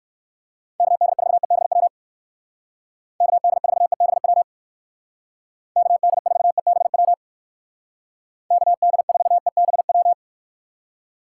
Morse code, heard as a keyed tone, about 59 wpm, 700 Hz. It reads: KD4EBW KD4EBW KD4EBW KD4EBW